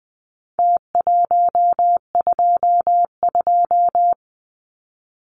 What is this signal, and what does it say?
Morse code, heard as a keyed tone, about 20 wpm, 700 Hz. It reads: T122
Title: Numbers - Contest Style - 4-Digit 20wpm